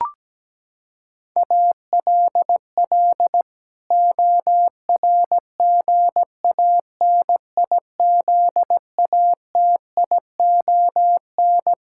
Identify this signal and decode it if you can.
Morse code, heard as a keyed tone, about 17 wpm, 700 Hz. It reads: ALL ORGANIZATION
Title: Sentences from Top 1000 Words - Review Part 5 - 17wpm